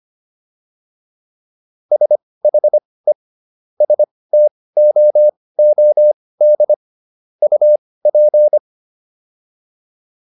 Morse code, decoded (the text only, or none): SHE STOOD UP